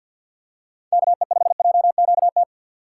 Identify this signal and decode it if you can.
Morse code, heard as a keyed tone, about 46 wpm, 690 Hz. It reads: KE5JQT
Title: USA Call Signs - 6 Characters - 3x Character Spacing - 50wpm